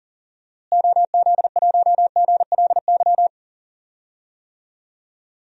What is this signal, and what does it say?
Morse code, heard as a keyed tone, about 40 wpm, 700 Hz. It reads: OZ1GLY
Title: DXCC-100 Call Signs - 6 Characters - Repeated 2x - 40wpm